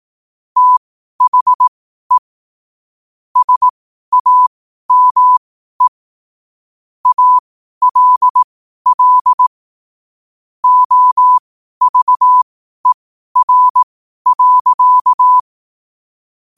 Morse code, heard as a keyed tone, about 18 wpm, spaced as for 12 wpm, 1000 Hz. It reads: THE SAME ALL OVER.
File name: qotdfiles / 12wpm